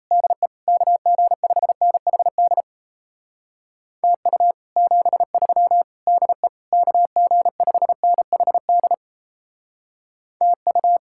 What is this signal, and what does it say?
Morse code, heard as a keyed tone, about 34 wpm, 700 Hz. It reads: DE KG5NHD TU 73 DE KG5NHD TU
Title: Speed-Racing Common QSO Phrases - 28wpm